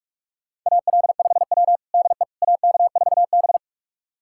Morse code, heard as a keyed tone, about 42 wpm, 700 Hz. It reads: AL5W DE AK4B
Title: USA Callsign DE Callsign - 4 Characters - 45wpm